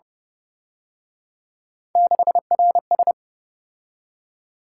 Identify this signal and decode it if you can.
Morse code, heard as a keyed tone, about 31 wpm, 700 Hz. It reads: 6RS